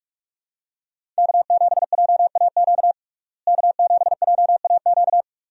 Morse code, heard as a keyed tone, about 46 wpm, 700 Hz. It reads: K7JAQ K7JAQ
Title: USA Call Signs - 5 Characters - Repeated 2x - 45wpm